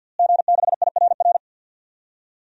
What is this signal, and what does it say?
Morse code, heard as a keyed tone, about 51 wpm, 700 Hz. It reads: G6IRR
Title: DXCC-100 Call Signs - 5 Characters - Repeated 3x - 50wpm